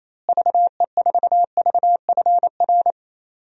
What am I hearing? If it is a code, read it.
Morse code, heard as a keyed tone, about 28 wpm, 700 Hz. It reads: VE4VFR
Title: DXCC-100 Call Signs - 6 Characters - Repeated 2x - 28wpm